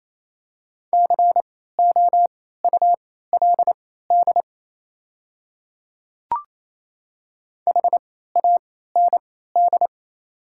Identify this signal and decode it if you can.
Morse code, heard as a keyed tone, about 28 wpm, spaced as for 16 wpm, 700 Hz. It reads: COULD HAND
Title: 3x Character Spacing - Top 200 Words - 28wpm